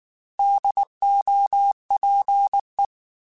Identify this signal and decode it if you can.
Morse code, heard as a keyed tone, about 19 wpm, 800 Hz. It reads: DOPE